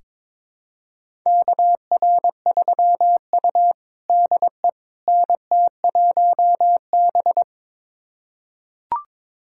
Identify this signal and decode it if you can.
Morse code, heard as a keyed tone, about 22 wpm, 700 Hz. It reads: KR3U DE NT1B
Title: USA Callsign DE Callsign - 4 Characters - 22wpm